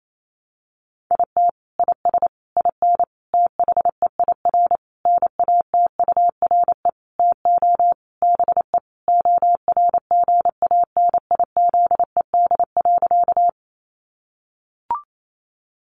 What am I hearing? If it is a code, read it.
Morse code, heard as a keyed tone, about 28 wpm, 700 Hz. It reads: IT IS IN THEIR NATURE TO BE ORGANIZED.
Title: Sentences from Top 1300 Words 28wpm